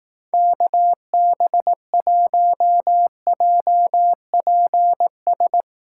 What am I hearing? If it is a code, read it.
Morse code, heard as a keyed tone, about 18 wpm, 700 Hz. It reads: KB1JPS
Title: Speed Racing - USA 2x3 Call Signs 15wpm